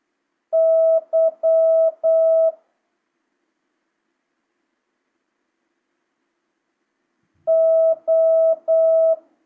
Morse code, heard as a keyed tone, about 8 wpm, 650 Hz. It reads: Y O